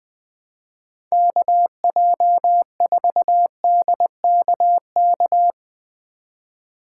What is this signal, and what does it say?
Morse code, heard as a keyed tone, about 22 wpm, 700 Hz. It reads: KJ4DKK
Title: Speed Racing - USA 2x3 Call Signs 20wpm